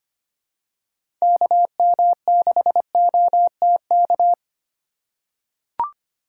Morse code, heard as a keyed tone, about 25 wpm, 700 Hz. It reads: KM6OTK
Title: USA Call Signs - 2x Character Spacing - 25wpm